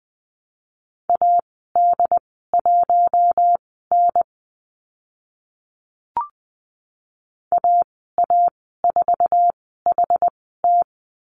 Morse code, heard as a keyed tone, about 20 wpm, spaced as for 14 wpm, 700 Hz. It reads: AD1N AA4HT
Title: 076 Lesson 13g - Letter L - Call Signs 20wpm